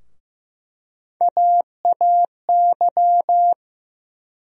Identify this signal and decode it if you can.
Morse code, heard as a keyed tone, about 15 wpm, 700 Hz. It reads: AAY
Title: USA Call Sign Postfixes - 3 Letters 15wpm